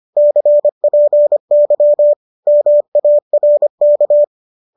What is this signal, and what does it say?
Morse code, heard as a keyed tone, about 25 wpm, 590 Hz. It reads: CPY MARK
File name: qso213 25